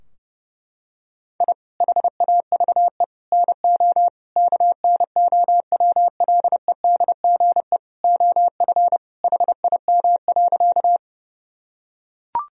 Morse code, heard as a keyed tone, about 30 wpm, 700 Hz. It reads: I HAVE NO KNOWLEDGE OF HIM.
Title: Sentences from Top 1000 Words - Review Part 3 - 30wpm